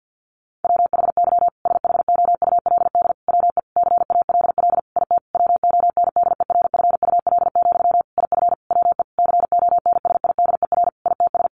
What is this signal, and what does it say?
Morse code, heard as a keyed tone, about 50 wpm, 700 Hz. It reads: WHY SHOULD WE CALL IT WONDERFUL, IF WE CONSIDER ITS
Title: Sentences from Top 1100 Words 50wpm